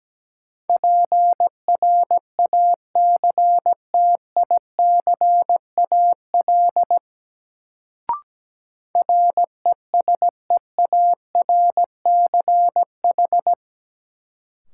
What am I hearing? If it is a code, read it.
Morse code, heard as a keyed tone, about 17 wpm, 700 Hz. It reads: PRACTICAL RESEARCH
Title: Sentences from Top 1400 Words - New Word Drill Alt - 17wpm